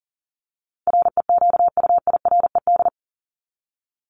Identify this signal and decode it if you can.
Morse code, heard as a keyed tone, about 40 wpm, 700 Hz. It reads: REQUIRED
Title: ED Suffix - Top 100 Words with Base Words 40wpm